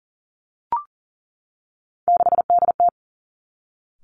Morse code, heard as a keyed tone, about 38 wpm, 700 Hz. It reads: 6DT
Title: Three Challenging Letter-Number Combinations 40wpm